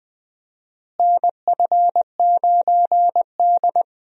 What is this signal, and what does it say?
Morse code, heard as a keyed tone, about 21 wpm, 700 Hz. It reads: NF9D
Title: USA Call Signs - 4 Characters - Repeated 2x - 20wpm